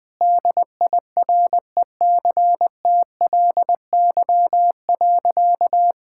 Morse code, heard as a keyed tone, about 20 wpm, 700 Hz. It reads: DIRECTLY.
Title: Sentences from Top 1000 Words - Review Part 5 - 20wpm